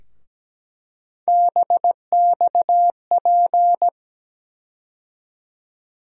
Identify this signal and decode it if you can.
Morse code, heard as a keyed tone, about 17 wpm, 700 Hz. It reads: BXP
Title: USA Call Sign Postfixes - 3 Letters 17wpm